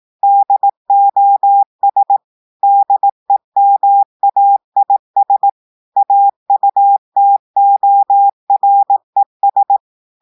Morse code, heard as a keyed tone, about 18 wpm, 800 Hz. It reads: DOS DEMAIS AUTORES
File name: audios / portugais / portugais texte en clair 1 vitesse 18 mots minutes